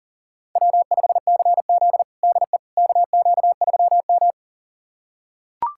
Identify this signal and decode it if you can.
Morse code, heard as a keyed tone, about 37 wpm, 700 Hz. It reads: W5CZ DE KQ3M
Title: USA Callsign DE Callsign - 4 Characters - 40wpm